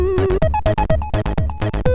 There is no code.